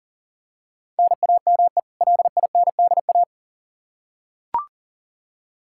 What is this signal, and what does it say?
Morse code, heard as a keyed tone, about 40 wpm, 700 Hz. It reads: NAME LINDA
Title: Practice QSO - Exchange 1 - Beginning - 40wpm